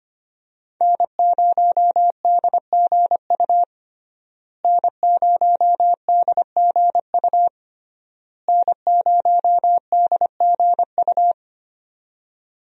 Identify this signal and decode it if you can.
Morse code, heard as a keyed tone, about 25 wpm, 700 Hz. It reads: N0DGU N0DGU N0DGU
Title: USA Call Signs - 5 Characters - Repeated 3x - 25wpm